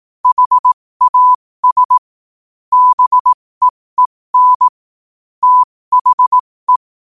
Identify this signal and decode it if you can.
Morse code, heard as a keyed tone, about 18 wpm, spaced as for 15 wpm, 1000 Hz. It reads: HAS BEEN THE
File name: qotdfiles / 15wpm